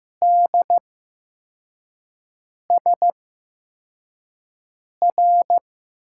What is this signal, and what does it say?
Morse code, heard as a keyed tone, about 15 wpm, spaced as for 4 wpm, 700 Hz. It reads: DSR
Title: USA Call Signs - 6 Characters - 8x Character Spacing - 15wpm